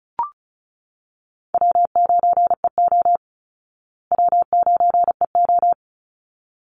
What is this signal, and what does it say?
Morse code, heard as a keyed tone, about 34 wpm, 700 Hz. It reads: W9EO W9EO
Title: Easy Call Signs Repeated 2x - 35wpm